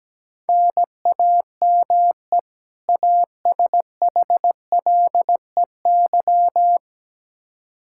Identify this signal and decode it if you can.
Morse code, heard as a keyed tone, about 17 wpm, 700 Hz. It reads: NAME ASHLEY
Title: Practice QSO - Exchange 1 - Beginning - 17wpm